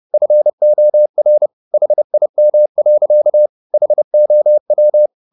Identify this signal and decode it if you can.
Morse code, heard as a keyed tone, about 30 wpm, 600 Hz. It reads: FOR HIM. HOW